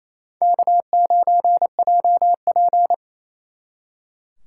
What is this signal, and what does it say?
Morse code, heard as a keyed tone, about 35 wpm, 700 Hz. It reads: K9JP
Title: Speed Racing - USA Call Signs 28wpm